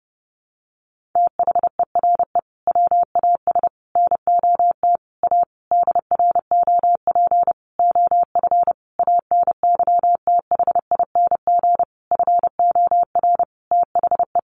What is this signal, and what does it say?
Morse code, heard as a keyed tone, about 30 wpm, 700 Hz. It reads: THERE WAS NOT A DROP OF ANYTHING FOR THE
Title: Sentences from Top 1500 Words 30wpm